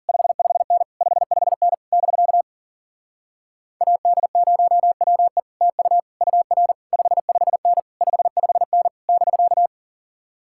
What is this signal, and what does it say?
Morse code, heard as a keyed tone, about 50 wpm, 700 Hz. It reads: 55N 55N <BK> AD0WE TU UR 55N 55N <BK>
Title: Speed-Racing Common QSO Phrases - 40wpm